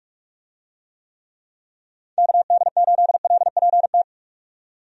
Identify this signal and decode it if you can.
Morse code, heard as a keyed tone, about 43 wpm, 700 Hz. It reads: KD8LPT
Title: USA Call Signs - 6 Characters - 45wpm